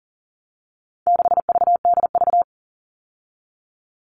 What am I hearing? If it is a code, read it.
Morse code, heard as a keyed tone, about 40 wpm, 700 Hz. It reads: 6VDV
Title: All Cut Numbers - 4-Digit 40wpm